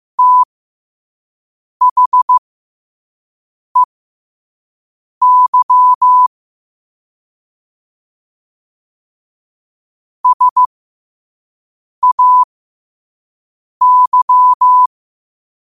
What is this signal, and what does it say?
Morse code, heard as a keyed tone, about 15 wpm, spaced as for 5 wpm, 1000 Hz. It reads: THEY SAY
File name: qotdfiles / 5wpm